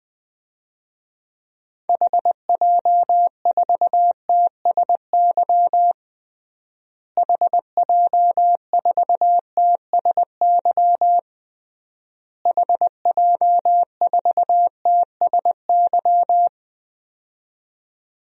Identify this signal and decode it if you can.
Morse code, heard as a keyed tone, about 20 wpm, 700 Hz. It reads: HJ4TSY HJ4TSY HJ4TSY
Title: DXCC-100 Call Signs - 6 Characters - Repeated 3x - 20wpm